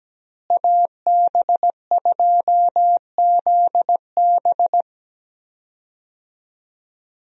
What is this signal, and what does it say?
Morse code, heard as a keyed tone, about 17 wpm, 700 Hz. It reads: AB2ZB
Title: USA Call Signs - 5 Characters - Repeated 3x - 17wpm